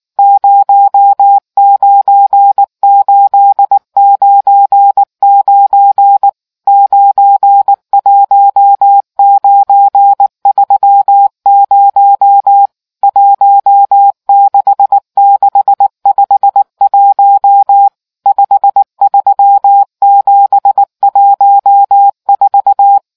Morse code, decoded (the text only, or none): 09899 91930 16651 53714